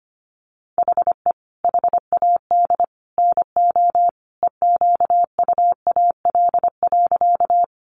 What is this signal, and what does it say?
Morse code, heard as a keyed tone, about 25 wpm, 700 Hz. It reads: HE HAD NO EQUAL.
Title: Sentences from Top 1400 Words 25wpm